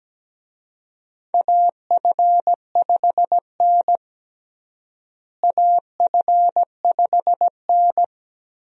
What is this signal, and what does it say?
Morse code, heard as a keyed tone, about 17 wpm, 700 Hz. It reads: AF5N AF5N
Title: USA Call Signs - 4 Characters - Repeated 2x - 17wpm